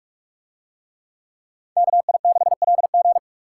Morse code, heard as a keyed tone, about 42 wpm, 700 Hz. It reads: KI6LG